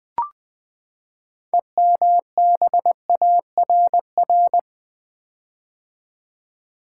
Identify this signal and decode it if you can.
Morse code, heard as a keyed tone, about 20 wpm, 700 Hz. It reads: EMBARR
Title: Build A Word - 6 Letters At A Time - 12-Letter Words - 20wpm